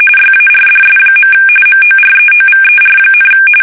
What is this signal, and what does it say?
RTTY, about 45.45 baud. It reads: YARDS. MOREOVER,
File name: fsk850 / battleshipnj